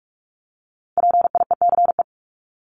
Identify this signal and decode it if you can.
Morse code, heard as a keyed tone, about 44 wpm, 700 Hz. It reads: PIECE
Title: Top 500 Words 45wpm